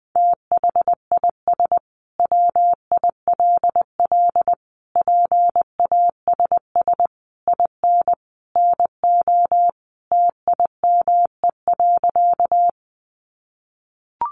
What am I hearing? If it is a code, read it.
Morse code, heard as a keyed tone, about 20 wpm, 700 Hz. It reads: THIS WILL PASS IN NO TIME.
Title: Sentences from Top 400 Words 20wpm